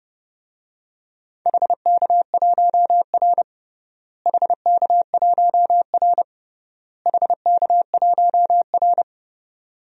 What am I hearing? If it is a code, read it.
Morse code, heard as a keyed tone, about 30 wpm, 700 Hz. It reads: HK1R HK1R HK1R
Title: DXCC-100 Call Signs - 4 Characters - Repeated 3x - 30wpm